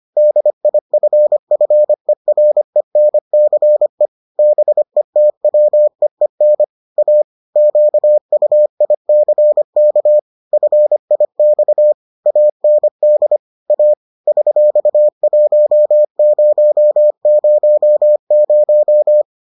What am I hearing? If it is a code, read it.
Morse code, heard as a keyed tone, about 25 wpm, 600 Hz. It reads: DIFFERENCE BETWEEN A QUICK FIX AND A $1000